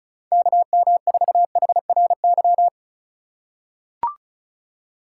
Morse code, heard as a keyed tone, about 34 wpm, 700 Hz. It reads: KM4HRY
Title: USA Call Signs - 6 Characters - 3x Character Spacing - 35wpm